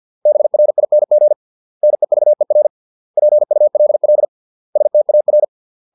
Morse code, heard as a keyed tone, about 50 wpm, 600 Hz. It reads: BRING NEVER PULL STAR